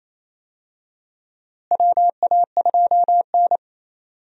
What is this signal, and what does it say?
Morse code, heard as a keyed tone, about 28 wpm, 700 Hz. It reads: WA2N
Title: USA Call Signs - 4 Characters - Repeated 2x - 28wpm